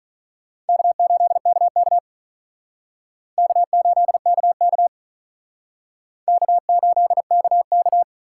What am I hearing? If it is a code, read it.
Morse code, heard as a keyed tone, about 42 wpm, 700 Hz. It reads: K8KK K8KK K8KK
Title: Speed Racing - USA 1x2 Call Signs 35wpm